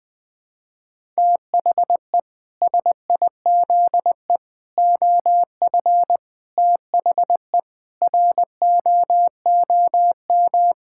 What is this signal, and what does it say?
Morse code, heard as a keyed tone, about 20 wpm, 700 Hz. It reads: THE SIZE OF THE ROOM
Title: Sentences with 4-Word Phrases - 20wpm